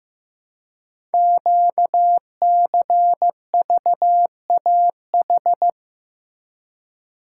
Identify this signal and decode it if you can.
Morse code, heard as a keyed tone, about 15 wpm, 700 Hz. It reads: QCVAH
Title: Code Groups - 5 Letters - 15wpm